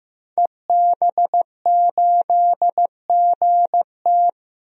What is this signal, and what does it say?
Morse code, heard as a keyed tone, about 15 wpm, 700 Hz. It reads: EB8GT
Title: DXCC-100 Call Signs - 5 Characters - 15wpm